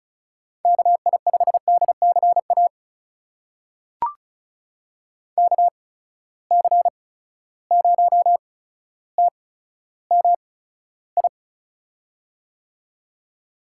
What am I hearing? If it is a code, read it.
Morse code, heard as a keyed tone, about 36 wpm, spaced as for 15 wpm, 700 Hz. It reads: KI5DCA KC0TMI
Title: USA Call Signs - 8x Character Spacing - 35wpm